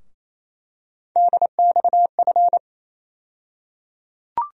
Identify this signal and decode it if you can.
Morse code, heard as a keyed tone, about 27 wpm, 700 Hz. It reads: DXF
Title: USA Call Sign Postfixes - 3 Letters 28wpm